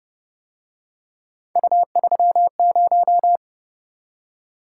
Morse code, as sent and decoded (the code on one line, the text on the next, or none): ..- ...-- -----
U30